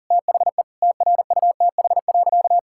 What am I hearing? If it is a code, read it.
Morse code, heard as a keyed tone, about 37 wpm, 700 Hz. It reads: THE TRUTH.
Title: Sentences from Top 1600 Words 40wpm